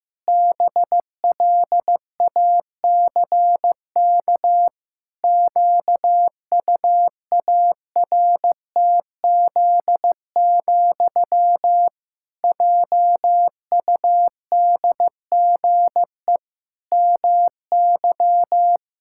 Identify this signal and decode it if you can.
Morse code, heard as a keyed tone, about 15 wpm, 700 Hz. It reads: BLACK QUARTZ, JUDGE MY
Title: Pangram Challenge v2 - 15wpm